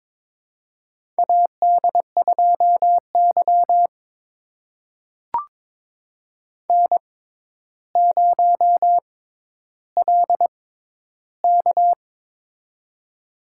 Morse code, heard as a keyed tone, about 22 wpm, spaced as for 11 wpm, 700 Hz. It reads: AD2Y N0LK
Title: USA Call Signs - 4 Characters - 6x Character Spacing - 22wpm